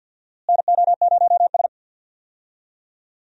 Morse code, heard as a keyed tone, about 49 wpm, 700 Hz. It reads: NO0S
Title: USA Call Signs - 4 Characters - Repeated 3x - 50wpm